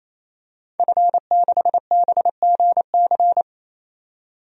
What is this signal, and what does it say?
Morse code, heard as a keyed tone, about 28 wpm, 700 Hz. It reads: F6BGC
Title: DXCC-100 Call Signs - 5 Characters - 28wpm